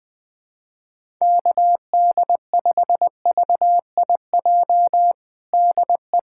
Morse code, heard as a keyed tone, about 21 wpm, 700 Hz. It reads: KD5VIJ DE
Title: USA Callsign DE Callsign - 6 Characters - 20wpm